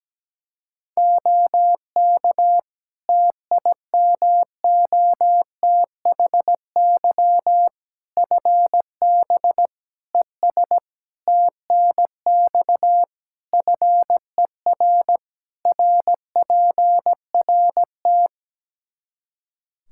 Morse code, heard as a keyed tone, about 17 wpm, 700 Hz. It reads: OK TIMOTHY FB ES TNX FER RPRT
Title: Practice QSO - Exchange 2 - Info - 17wpm